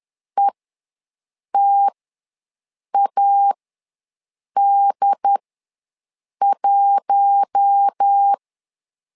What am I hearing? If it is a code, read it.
Morse code, heard as a keyed tone, about 10 wpm, spaced as for 6 wpm, 800 Hz. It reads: ETAD1
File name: LEZIONI / 4 Q1 ws 10wpm 2538